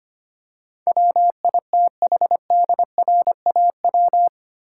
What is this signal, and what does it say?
Morse code, heard as a keyed tone, about 25 wpm, 700 Hz. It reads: WITHDRAW
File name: 8 Letter Words - First 3 Letters Given - 25wpm